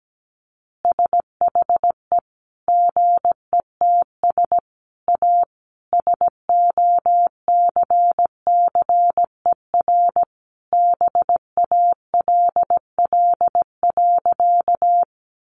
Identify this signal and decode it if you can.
Morse code, heard as a keyed tone, about 17 wpm, 700 Hz. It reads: SHE GETS A SOCCER BALL.
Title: Easy Phrases 17wpm